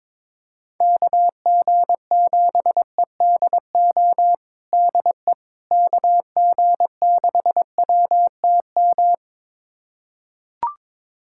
Morse code, heard as a keyed tone, about 21 wpm, 700 Hz. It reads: KG7EDO DE KG6WTM
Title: USA Callsign DE Callsign 22wpm